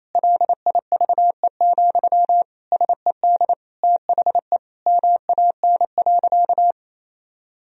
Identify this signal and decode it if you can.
Morse code, heard as a keyed tone, about 28 wpm, 700 Hz. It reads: LIVE, SED THE MAN.
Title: Sentences from Top 200 Words and Abbreviations 28wpm